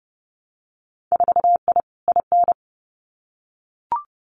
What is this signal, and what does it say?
Morse code, heard as a keyed tone, about 30 wpm, 700 Hz. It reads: 4I IN
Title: Winter Field Day Exchanges Encore 30wpm